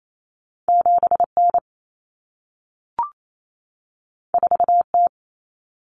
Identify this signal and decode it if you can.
Morse code, heard as a keyed tone, about 28 wpm, 700 Hz. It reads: 7N 4T
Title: Common Cut Numbers - 2-Digit 28wpm